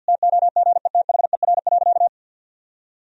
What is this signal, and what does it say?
Morse code, heard as a keyed tone, about 52 wpm, 700 Hz. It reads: TOGETHER.
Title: Sentences from Top 1000 Words - Review Part 2 - 50wpm